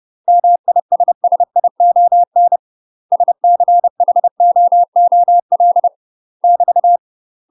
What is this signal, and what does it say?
Morse code, heard as a keyed tone, about 30 wpm, 700 Hz. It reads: MISSION SCHOOL <BT>
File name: UK MASTER QSOs TRAINING / 30WPM / UK MASTER QSOs 0000